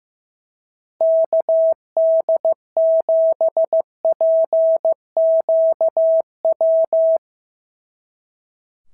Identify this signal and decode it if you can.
Morse code, heard as a keyed tone, about 15 wpm, 690 Hz. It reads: KD7PQW
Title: SOTA QSO Practice 15wpm